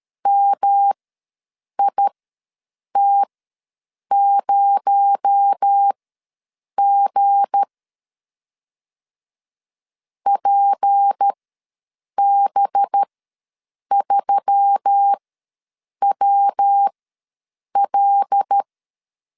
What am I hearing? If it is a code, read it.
Morse code, heard as a keyed tone, about 13 wpm, spaced as for 7 wpm, 800 Hz. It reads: MIT0G PB3WL
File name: LEZIONI / 7 GF9Z / 7 G ws 12wpm 3140